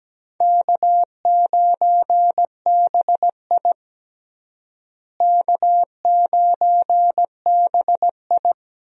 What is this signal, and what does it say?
Morse code, heard as a keyed tone, about 17 wpm, 700 Hz. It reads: K9BI K9BI
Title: USA Call Signs - 4 Characters - Repeated 2x - 17wpm